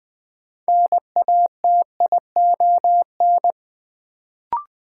Morse code, heard as a keyed tone, about 20 wpm, 700 Hz. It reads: NATION
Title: Sets of 3 Words - New Word Drill - 20wpm